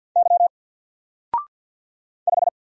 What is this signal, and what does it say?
Morse code, heard as a keyed tone, about 50 wpm, 700 Hz. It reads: Y 5
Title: Single Letter-Number Alt 50wpm